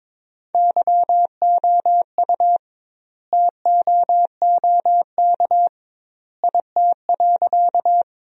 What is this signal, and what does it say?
Morse code, heard as a keyed tone, about 22 wpm, 700 Hz. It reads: YOU TOOK IT.
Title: 2x Word-Spacing - Sentences from Top 300 Words - 22wpm